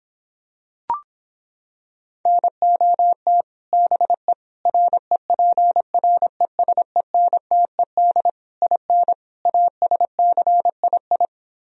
Morse code, heard as a keyed tone, about 26 wpm, 700 Hz. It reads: NOT BE REPRESENTED IN ASCII